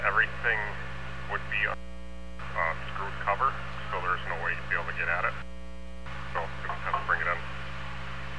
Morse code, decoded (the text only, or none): none